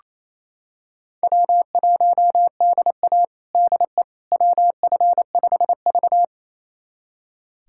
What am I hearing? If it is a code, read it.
Morse code, heard as a keyed tone, about 28 wpm, 700 Hz. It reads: W1DA DE WF5V
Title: USA Callsign DE Callsign - 4 Characters - 28wpm